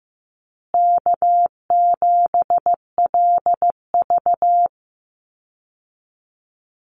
Morse code, heard as a keyed tone, about 15 wpm, 700 Hz. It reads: K7LV
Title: Call Signs Repeated 2x - 15wpm